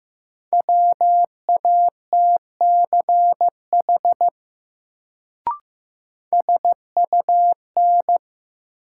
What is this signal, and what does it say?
Morse code, heard as a keyed tone, about 15 wpm, 700 Hz. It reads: WATCH SUN
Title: Sentences from Top 300 Words - New Word Drill - 15wpm